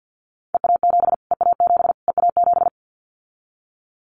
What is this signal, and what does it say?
Morse code, heard as a keyed tone, about 50 wpm, 700 Hz. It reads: EA7 EA7 EA7
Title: Callsign Prefixes for Top 25 DXCC - Repeated 3x - 50wpm